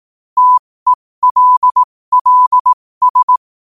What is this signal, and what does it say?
Morse code, heard as a keyed tone, about 18 wpm, spaced as for 15 wpm, 1000 Hz. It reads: TELLS
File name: qotdfiles / 15wpm